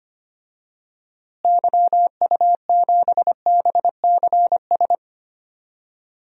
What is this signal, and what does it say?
Morse code, heard as a keyed tone, about 25 wpm, 700 Hz. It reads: YU7BCS
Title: DXCC-100 Call Signs - 6 Characters - Repeated 2x - 25wpm